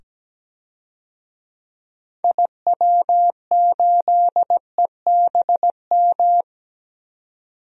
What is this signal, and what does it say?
Morse code, heard as a keyed tone, about 17 wpm, 700 Hz. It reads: IW8EBM
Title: DXCC-100 Call Signs - 6 Characters - 17wpm